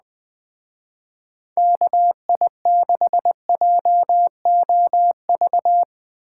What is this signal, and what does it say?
Morse code, heard as a keyed tone, about 20 wpm, 700 Hz. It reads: KI6JOV
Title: Speed Racing - USA Call Signs 17wpm